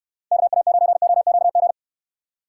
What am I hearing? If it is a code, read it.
Morse code, heard as a keyed tone, about 66 wpm, 700 Hz. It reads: KA9OQG